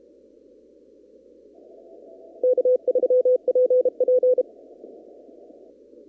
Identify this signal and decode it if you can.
Morse code, heard as a keyed tone, about 32 wpm, 500 Hz. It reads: K3PP
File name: CW / kt5j wpx cw-op-k5pi